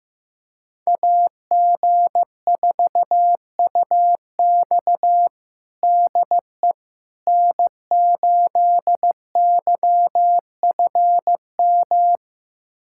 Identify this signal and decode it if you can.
Morse code, heard as a keyed tone, about 15 wpm, 700 Hz. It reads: AG4UX DE N8YFM
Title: USA Callsign DE Callsign - 5 Characters - 15wpm